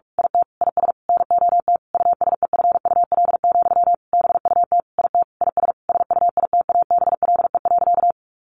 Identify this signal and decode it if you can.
Morse code, heard as a keyed tone, about 44 wpm, 700 Hz. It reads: IT IS NOT USEFUL, BUT IT IS SUITABLE.
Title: Sentences from Top 1400 Words 45wpm